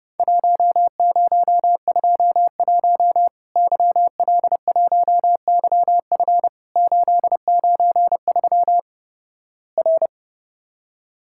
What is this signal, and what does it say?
Morse code, heard as a keyed tone, about 32 wpm, 700 Hz. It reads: 1021 YL1YF 893 R
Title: Worked All Europe DX Contest - QTC Exchange - 30wpm